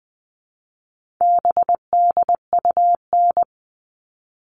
Morse code, heard as a keyed tone, about 20 wpm, 700 Hz. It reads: BDUN
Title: All Cut Numbers - 4-Digit 20wpm